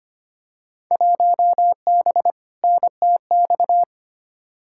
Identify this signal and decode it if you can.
Morse code, heard as a keyed tone, about 25 wpm, 700 Hz. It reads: 1B NTX
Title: Field Day Exchanges - 25wpm